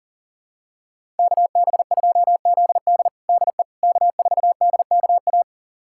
Morse code, heard as a keyed tone, about 37 wpm, 700 Hz. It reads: KB2ZD DE K4DKA
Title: USA Callsign DE Callsign - 5 Characters - 40wpm